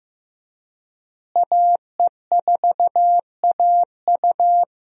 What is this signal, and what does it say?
Morse code, heard as a keyed tone, about 15 wpm, 700 Hz. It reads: AE4AU
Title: USA Call Signs - 5 Characters - Repeated 2x - 15wpm